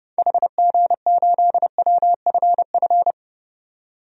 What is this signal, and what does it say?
Morse code, heard as a keyed tone, about 30 wpm, 700 Hz. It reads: HG8WFF
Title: DXCC-100 Call Signs - 6 Characters - 30wpm